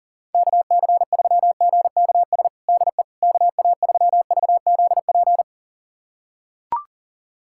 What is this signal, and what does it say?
Morse code, heard as a keyed tone, about 40 wpm, 700 Hz. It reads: KC3GKS DE KA3VZP
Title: USA Callsign DE Callsign - 6 Characters - 40wpm